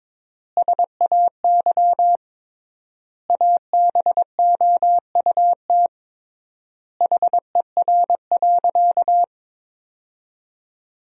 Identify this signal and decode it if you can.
Morse code, heard as a keyed tone, about 22 wpm, 700 Hz. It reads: SAY ABOUT HER.
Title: 3x Word-Spacing - Sentences from Top 200 Words - 22wpm